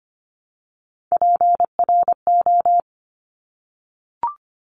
Morse code, heard as a keyed tone, about 25 wpm, 700 Hz. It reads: PRO
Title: Top Three Letter Combinations in Words 25wpm